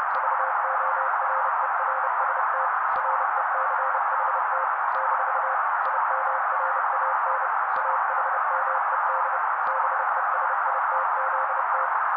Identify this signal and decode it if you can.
Morse code, heard as a keyed tone, about 30 wpm, 530 Hz. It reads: VORERST RECHT - IMRAN AHMED BESITZT
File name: daily / 29wpm noise